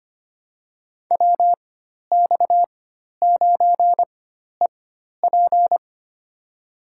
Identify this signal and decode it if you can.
Morse code, heard as a keyed tone, about 25 wpm, spaced as for 12 wpm, 700 Hz. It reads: WX9EP
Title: USA Call Signs - 5 Characters - 4x Character Spacing - 25wpm